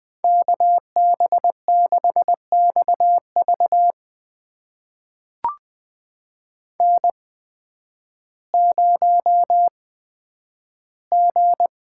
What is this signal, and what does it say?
Morse code, heard as a keyed tone, about 20 wpm, spaced as for 11 wpm, 700 Hz. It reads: KB6XV N0G
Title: USA Call Signs - 5 Characters - 8x Character Spacing - 20wpm